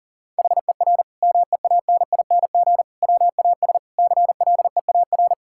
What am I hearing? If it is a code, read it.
Morse code, heard as a keyed tone, about 40 wpm, 700 Hz. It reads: HER MEANING WAS CLEAR
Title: Sentences from Top 1000 Words - Review Part 6 - 40wpm